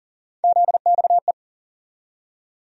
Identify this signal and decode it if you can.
Morse code, heard as a keyed tone, about 40 wpm, 700 Hz. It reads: ZXE
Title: USA Call Sign Postfixes - 3 Letters 40wpm